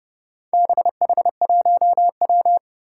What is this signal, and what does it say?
Morse code, heard as a keyed tone, about 29 wpm, 700 Hz. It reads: BH1W
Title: Four Challenging Letter-Number Combinations 30wpm